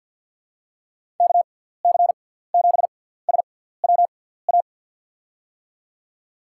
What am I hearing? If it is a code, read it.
Morse code, heard as a keyed tone, about 51 wpm, spaced as for 17 wpm, 700 Hz. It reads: KC7SWA
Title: USA Call Signs - 6 Characters - 6x Character Spacing - 50wpm